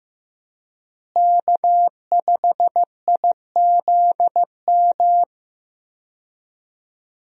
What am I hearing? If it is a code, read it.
Morse code, heard as a keyed tone, about 15 wpm, 700 Hz. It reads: K5IZM
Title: USA Call Signs - 5 Characters - Repeated 3x - 15wpm